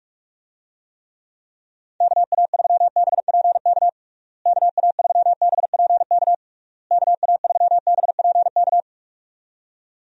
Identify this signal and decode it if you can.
Morse code, heard as a keyed tone, about 46 wpm, 700 Hz. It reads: KA3BPK KA3BPK KA3BPK
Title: USA Call Signs - 6 Characters - Repeated 3x - 45wpm